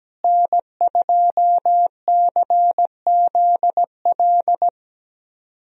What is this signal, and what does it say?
Morse code, heard as a keyed tone, about 17 wpm, 700 Hz. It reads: N2CZL
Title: USA Call Signs - 5 Characters - Repeated 2x - 17wpm